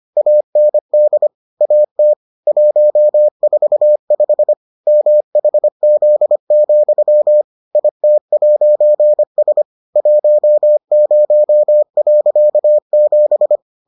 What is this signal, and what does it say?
Morse code, heard as a keyed tone, about 25 wpm, 600 Hz. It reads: AND AT 145 MHZ, IT'S 10.7